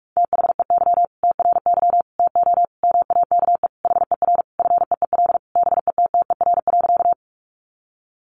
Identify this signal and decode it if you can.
Morse code, heard as a keyed tone, about 44 wpm, 700 Hz. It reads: THEY TRY TO MAKE HER FEEL BETTER.
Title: Easy Phrases 45wpm